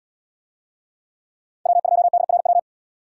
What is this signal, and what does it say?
Morse code, heard as a keyed tone, about 69 wpm, 700 Hz. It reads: W2DRW